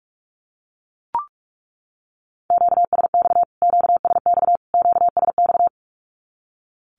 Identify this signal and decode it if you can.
Morse code, heard as a keyed tone, about 44 wpm, 700 Hz. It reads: QSX QSX QSX
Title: Q-Codes Repeated 3x 45wpm